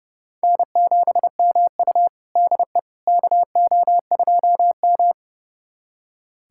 Morse code, heard as a keyed tone, about 30 wpm, 700 Hz. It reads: N7MU DE KO2M
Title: USA Callsign DE Callsign - 4 Characters - 30wpm